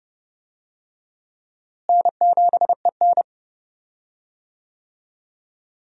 Morse code, heard as a keyed tone, about 31 wpm, 700 Hz. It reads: N7EN
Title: USA Call Signs - 4 Characters - Repeated 3x - 30wpm